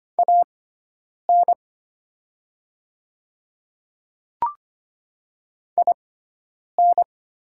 Morse code, heard as a keyed tone, about 25 wpm, spaced as for 9 wpm, 690 Hz. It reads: AN IN
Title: 6x Character Spacing - Top 100 Words - 25wpm